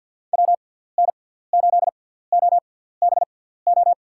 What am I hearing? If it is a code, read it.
Morse code, heard as a keyed tone, about 53 wpm, spaced as for 17 wpm, 700 Hz. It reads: WN8OBO
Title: USA Call Signs - 6 Characters - 6x Character Spacing - 50wpm